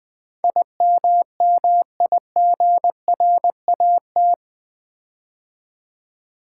Morse code, heard as a keyed tone, about 20 wpm, 700 Hz. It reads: IMMIGRAT
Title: Build A Word - 4 Letters At A Time - 11-Letter Words - 20wpm